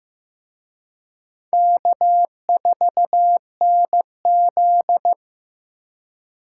Morse code, heard as a keyed tone, about 18 wpm, 700 Hz. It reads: K4NZ
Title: Speed Racing - USA 1x2 Call Signs 15wpm